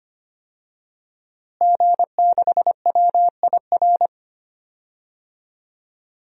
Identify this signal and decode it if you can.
Morse code, heard as a keyed tone, about 25 wpm, 700 Hz. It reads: G6WIR
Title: DXCC-100 Call Signs - 5 Characters - Repeated 2x - 25wpm